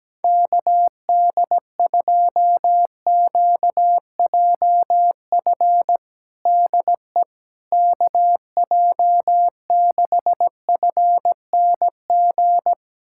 Morse code, heard as a keyed tone, about 17 wpm, 700 Hz. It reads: KD2QJF DE KJ6FNG
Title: USA Callsign DE Callsign - 6 Characters - 17wpm